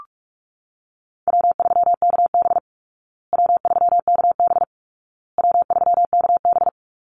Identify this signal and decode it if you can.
Morse code, heard as a keyed tone, about 44 wpm, 700 Hz. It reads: W3KB W3KB W3KB
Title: Easy Call Signs Repeated 3x - 45wpm